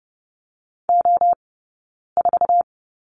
Morse code, heard as a keyed tone, about 30 wpm, 700 Hz. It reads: O 4
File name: 092 Lesson 16d - Letter c - Two Characters 30wpm